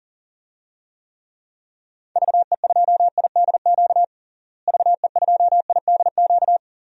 Morse code, heard as a keyed tone, about 37 wpm, 700 Hz. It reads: VE2IDQ VE2IDQ
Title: DXCC-100 Call Signs - 6 Characters - Repeated 2x - 40wpm